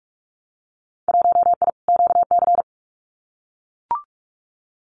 Morse code, heard as a keyed tone, about 44 wpm, 700 Hz. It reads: 1I QC
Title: Winter Field Day Exchanges 45wpm